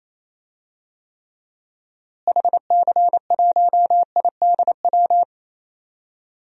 Morse code, heard as a keyed tone, about 28 wpm, 700 Hz. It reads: HC1IDW
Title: DXCC-100 Call Signs - 6 Characters - Repeated 3x - 28wpm